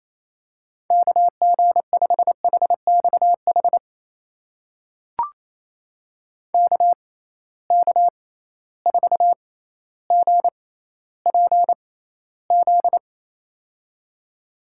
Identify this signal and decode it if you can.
Morse code, heard as a keyed tone, about 28 wpm, spaced as for 14 wpm, 700 Hz. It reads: KG5HXH KK4GPZ
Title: USA Call Signs - 6x Character Spacing - 28wpm